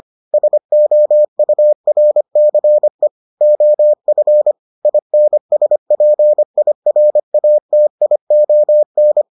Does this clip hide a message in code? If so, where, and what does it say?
Yes: Morse code from 0.3 to 9.2 s, heard as a keyed tone, about 25 wpm, 600 Hz. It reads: SOURCE OF INSPIRATION